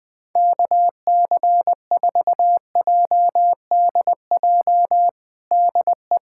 Morse code, heard as a keyed tone, about 21 wpm, 700 Hz. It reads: KC4JDJ DE
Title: USA Callsign DE Callsign - 6 Characters - 20wpm